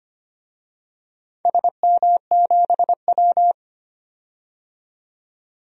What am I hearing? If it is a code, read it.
Morse code, heard as a keyed tone, about 25 wpm, 700 Hz. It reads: SM7W
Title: DXCC-100 Call Signs - 4 Characters - 25wpm